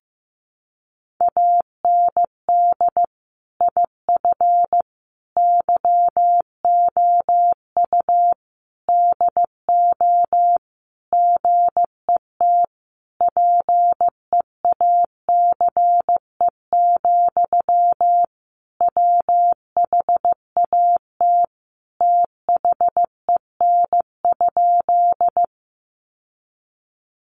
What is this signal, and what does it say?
Morse code, heard as a keyed tone, about 15 wpm, 700 Hz. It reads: AND IF YOU DO GET PEACE, WHAT THEN?
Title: Sentences from Top 1500 Words 15wpm